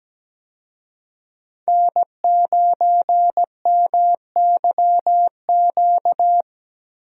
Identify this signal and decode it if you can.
Morse code, heard as a keyed tone, about 17 wpm, 700 Hz. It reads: N9MYQ
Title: USA Call Signs - 5 Characters - 17wpm